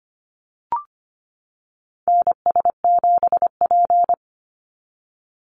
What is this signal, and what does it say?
Morse code, heard as a keyed tone, about 25 wpm, 700 Hz. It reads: NS7P
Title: NA SOTA Activator Callsigns 25wpm